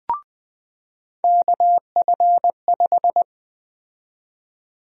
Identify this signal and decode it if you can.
Morse code, heard as a keyed tone, about 21 wpm, 700 Hz. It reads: KF5
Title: USA Call Sign Prefixes - 20wpm